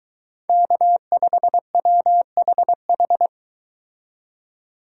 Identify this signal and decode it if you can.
Morse code, heard as a keyed tone, about 22 wpm, 700 Hz. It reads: K5WHH